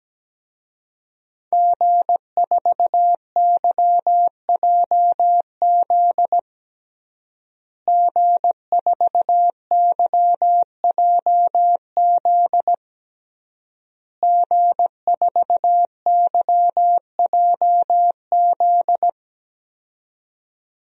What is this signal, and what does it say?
Morse code, heard as a keyed tone, about 17 wpm, 700 Hz. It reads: G4YJZ G4YJZ G4YJZ
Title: DXCC-100 Call Signs - 5 Characters - Repeated 3x - 17wpm